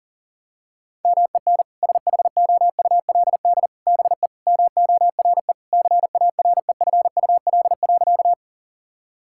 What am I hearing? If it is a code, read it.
Morse code, heard as a keyed tone, about 40 wpm, 700 Hz. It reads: MEN SHOULD BE MORE CAREFUL.
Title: Sentences from Top 1600 Words 40wpm